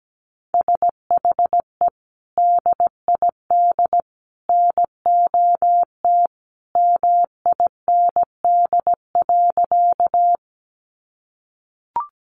Morse code, heard as a keyed tone, about 17 wpm, 700 Hz. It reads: SHE DID NOT MIND.
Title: Sentences from Top 600 Words 17wpm